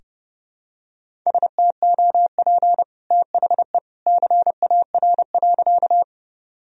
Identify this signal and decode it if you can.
Morse code, heard as a keyed tone, about 30 wpm, 700 Hz. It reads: STOP THE CAR.
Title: Sentences from Top 1000 Words - Review Part 1 - 30wpm